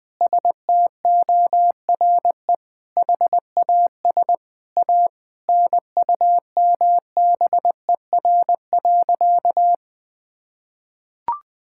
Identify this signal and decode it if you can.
Morse code, heard as a keyed tone, about 20 wpm, 700 Hz. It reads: STORE HAS A NUMBER.
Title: Sentences from Top 1000 Words - Review Part 6 - 20wpm